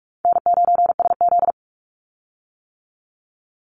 Morse code, heard as a keyed tone, about 44 wpm, 700 Hz. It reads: N9SZ
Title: Easy Call Signs - 45wpm